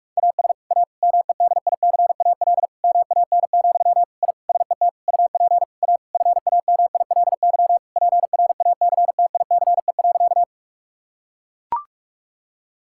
Morse code, heard as a keyed tone, about 42 wpm, 700 Hz. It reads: AS A MEDICAL MAN, I SET UP A FAMILY PRACTICE.
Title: Sentences from Top 1000 Words - Review Part 5 - 45wpm